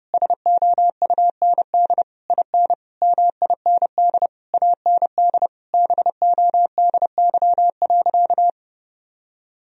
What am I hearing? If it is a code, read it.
Morse code, heard as a keyed tone, about 30 wpm, 700 Hz. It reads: SOUND IN MIND AND BODY.
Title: Sentences from Top 1000 Words - Review Part 2 - 30wpm